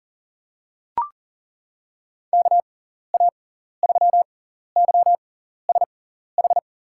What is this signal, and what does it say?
Morse code, heard as a keyed tone, about 40 wpm, spaced as for 14 wpm, 700 Hz. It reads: KA3YSH
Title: USA Call Signs - 6x Character Spacing - 40wpm